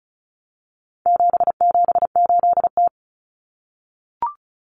Morse code, heard as a keyed tone, about 34 wpm, 700 Hz. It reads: 778T
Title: Common Cut Numbers - 4-Digit 35wpm